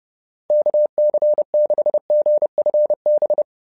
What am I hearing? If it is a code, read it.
Morse code, heard as a keyed tone, about 30 wpm, 600 Hz. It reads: KC6GFB
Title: SOTA QSO Practice 30wpm